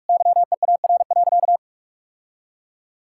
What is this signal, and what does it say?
Morse code, heard as a keyed tone, about 41 wpm, 700 Hz. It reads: YEAR.
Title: Sentences from Top 1000 Words - Review Part 5 - 45wpm